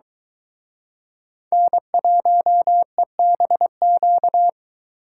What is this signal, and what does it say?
Morse code, heard as a keyed tone, about 22 wpm, 700 Hz. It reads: N1EBQ